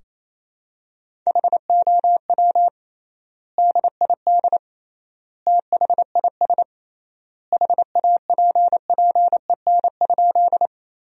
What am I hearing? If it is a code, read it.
Morse code, heard as a keyed tone, about 27 wpm, 700 Hz. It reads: HOW DID THIS HAPPEN?
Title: 3x Word-Spacing - Sentences from Top 400 Words - 28wpm